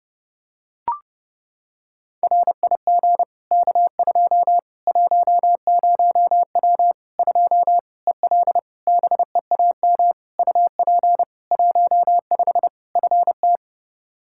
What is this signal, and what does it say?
Morse code, heard as a keyed tone, about 30 wpm, 700 Hz. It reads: RIG K2 10W 2 EL BEAM UP 15 FT
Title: Practice QSO - Exchange 2 - Info - 30wpm